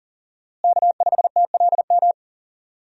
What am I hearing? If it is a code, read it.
Morse code, heard as a keyed tone, about 40 wpm, 700 Hz. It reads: K5TLM
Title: USA Call Signs - 5 Characters - Repeated 3x - 40wpm